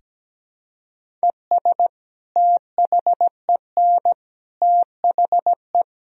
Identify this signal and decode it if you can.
Morse code, heard as a keyed tone, about 17 wpm, 700 Hz. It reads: ES THEN THE
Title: Sentences from Top 500 Words and Abbreviations 17wpm